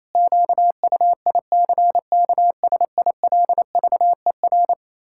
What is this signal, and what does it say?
Morse code, heard as a keyed tone, about 27 wpm, 700 Hz. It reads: QUICKSILVER
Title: Pangram Challenge v2 - 28wpm